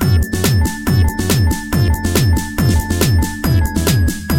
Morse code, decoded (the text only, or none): none